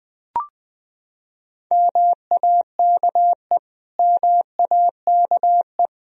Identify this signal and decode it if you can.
Morse code, heard as a keyed tone, about 20 wpm, 700 Hz. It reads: MAKE MAKE
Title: Top 100 Words Repeated 20wpm